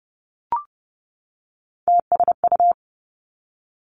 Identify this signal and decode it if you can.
Morse code, heard as a keyed tone, about 30 wpm, 700 Hz. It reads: TSU
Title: Three Challenging Letter-Number Combinations 30wpm